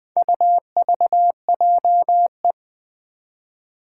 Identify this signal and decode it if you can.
Morse code, heard as a keyed tone, about 21 wpm, 700 Hz. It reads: UVJE
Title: Four Challenging Letter-Number Combinations 20wpm